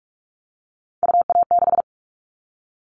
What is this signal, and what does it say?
Morse code, heard as a keyed tone, about 44 wpm, 700 Hz. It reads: UA6
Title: All Cut Numbers - 3-Digit 45wpm